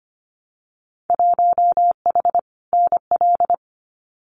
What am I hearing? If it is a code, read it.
Morse code, heard as a keyed tone, about 25 wpm, 700 Hz. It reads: 1H NL
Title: Winter Field Day Exchanges 25wpm